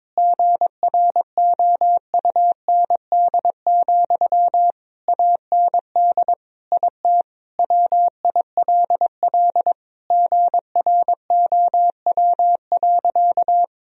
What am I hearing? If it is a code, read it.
Morse code, heard as a keyed tone, about 22 wpm, 700 Hz. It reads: GROUND, AND IT WILL GROW.
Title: Sentences from Top 1000 Words - Review Part 2 - 22wpm